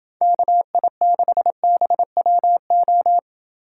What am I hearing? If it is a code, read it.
Morse code, heard as a keyed tone, about 30 wpm, 700 Hz. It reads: KI6BWO